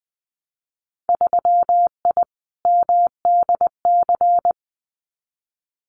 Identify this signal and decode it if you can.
Morse code, heard as a keyed tone, about 20 wpm, 700 Hz. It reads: 3I MDC
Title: Winter Field Day Exchanges Encore 20wpm